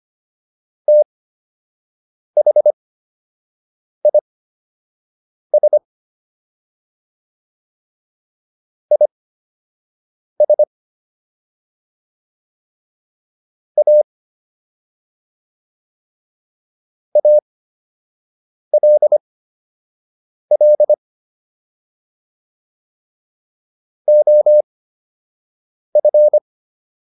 Morse code, heard as a keyed tone, about 25 wpm, spaced as for 6 wpm, 600 Hz. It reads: THIS IS A ALL OF